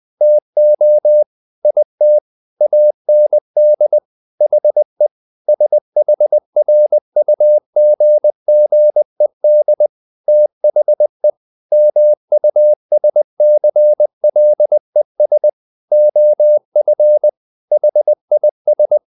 Morse code, heard as a keyed tone, about 20 wpm, 600 Hz. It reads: TO IT AND HE SHRUGGED THE MUSCLES OF HIS
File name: the old man and the sea / the old man and the sea part 1 20wpm